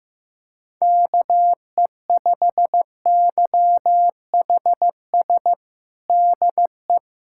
Morse code, heard as a keyed tone, about 15 wpm, 700 Hz. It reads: KE5YHS DE
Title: USA Callsign DE Callsign - 6 Characters - 15wpm